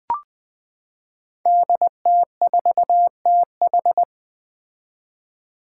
Morse code, heard as a keyed tone, about 20 wpm, 700 Hz. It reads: DT4TH
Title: Five Challenging Letter-Number Combinations 20wpm